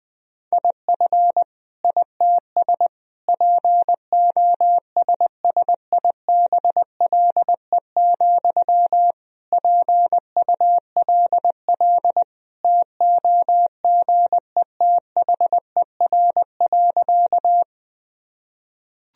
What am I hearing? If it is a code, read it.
Morse code, heard as a keyed tone, about 20 wpm, 700 Hz. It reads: IF ITS POSSIBLE, PULL TOGETHER.
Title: Sentences from Top 1000 Words - Review Part 2 - 20wpm